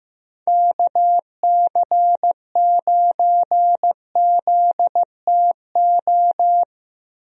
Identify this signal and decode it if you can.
Morse code, heard as a keyed tone, about 15 wpm, 700 Hz. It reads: KC9ZTO
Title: USA Call Signs - 6 Characters - Repeated 3x - 15wpm